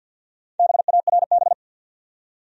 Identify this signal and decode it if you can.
Morse code, heard as a keyed tone, about 46 wpm, 700 Hz. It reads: BARB
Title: USA Common Names - Max 5 Letters - 50wpm